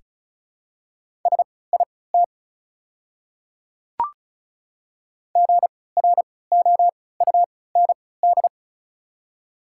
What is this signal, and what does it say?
Morse code, heard as a keyed tone, about 36 wpm, spaced as for 20 wpm, 690 Hz. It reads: SIT GROUND
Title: 3x Character Spacing - Top 500 Words - 35wpm